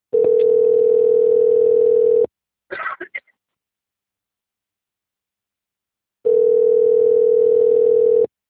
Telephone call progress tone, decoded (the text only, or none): ringback tone